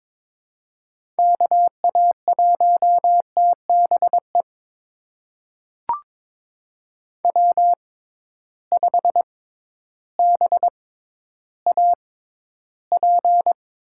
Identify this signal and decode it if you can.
Morse code, heard as a keyed tone, about 21 wpm, spaced as for 12 wpm, 700 Hz. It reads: KA1TBE W5BAP
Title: USA Call Signs - 6x Character Spacing - 22wpm